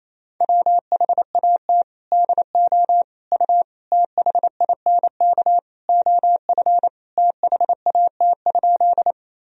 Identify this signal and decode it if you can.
Morse code, heard as a keyed tone, about 28 wpm, 700 Hz. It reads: WHAT DO U THINK OF THAT?
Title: Sentences from Top 200 Words and Abbreviations 28wpm